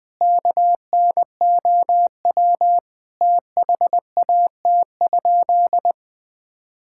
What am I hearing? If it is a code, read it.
Morse code, heard as a keyed tone, about 21 wpm, 700 Hz. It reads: KNOW THAT?
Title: Sentences with 4-Word Phrases - 20wpm